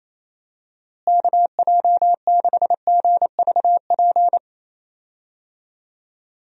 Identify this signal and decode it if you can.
Morse code, heard as a keyed tone, about 27 wpm, 700 Hz. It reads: KJ6GVP
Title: USA Call Signs - 6 Characters - Repeated 3x - 28wpm